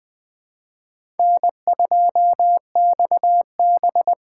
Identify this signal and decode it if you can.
Morse code, heard as a keyed tone, about 20 wpm, 700 Hz. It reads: N2XB
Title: USA Call Signs - 4 Characters - 4x Character Spacing - 20wpm